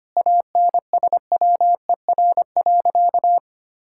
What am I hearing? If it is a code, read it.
Morse code, heard as a keyed tone, about 25 wpm, 700 Hz. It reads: ANSWER.
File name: practice / 25wpm / Sentences with 4-Word Phrases - 25wpm